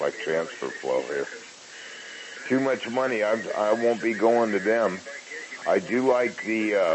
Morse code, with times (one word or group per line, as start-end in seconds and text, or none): none